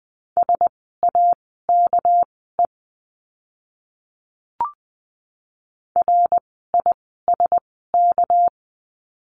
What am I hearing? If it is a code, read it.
Morse code, heard as a keyed tone, about 20 wpm, spaced as for 14 wpm, 700 Hz. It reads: SAKE RISK
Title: 199 Lesson 32f - Letter k - Words 20wpm